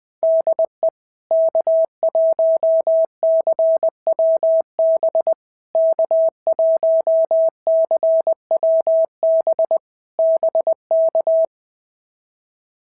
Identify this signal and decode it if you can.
Morse code, heard as a keyed tone, about 19 wpm, 650 Hz. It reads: DE K1CWB K1CWB BK
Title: POTA QSO Practice - 20wpm